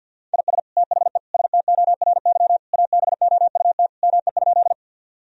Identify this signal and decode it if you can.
Morse code, heard as a keyed tone, about 51 wpm, 700 Hz. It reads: IS THE STORY ABOUT ME?
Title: Sentences from Top 1000 Words - Review Part 1 - 50wpm